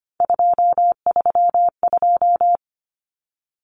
Morse code, heard as a keyed tone, about 25 wpm, 700 Hz. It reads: 232
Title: Common Cut Numbers - 3-Digit 25wpm